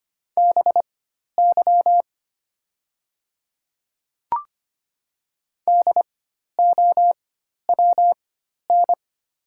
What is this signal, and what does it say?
Morse code, heard as a keyed tone, about 25 wpm, spaced as for 12 wpm, 700 Hz. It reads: BY DOWN
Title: 4x Character Spacing - Top 100 Words - 25wpm